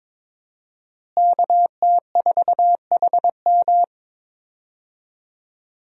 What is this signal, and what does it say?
Morse code, heard as a keyed tone, about 21 wpm, 700 Hz. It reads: KT4HM
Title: USA Call Signs - 5 Characters - Repeated 2x - 22wpm